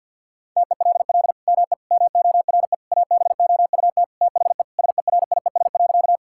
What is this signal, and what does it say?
Morse code, heard as a keyed tone, about 50 wpm, 700 Hz. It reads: TELL ME MORE ABOUT THE SERIES.
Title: Sentences from Top 1000 Words - Review Part 5 - 50wpm